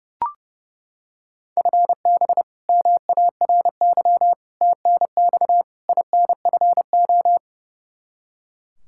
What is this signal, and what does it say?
Morse code, heard as a keyed tone, about 30 wpm, 700 Hz. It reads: FB MARY TNX INFO
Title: Practice QSO - Exchange 3 - More Info - 30wpm